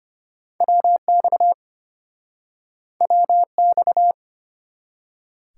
Morse code, heard as a keyed tone, about 34 wpm, 700 Hz. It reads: WX WX